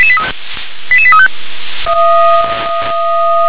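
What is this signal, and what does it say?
Five-tone selective call (ZVEI), heard as quick runs of five tones, tones of about 80 ms. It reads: 88835
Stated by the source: ZVEI five-tone selective call, received by SDR.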